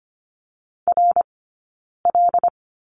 Morse code, heard as a keyed tone, about 25 wpm, 700 Hz. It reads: R L
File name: course / 25:10wpm / 092 Lesson 16d - Letter c - Two Characters 25wpm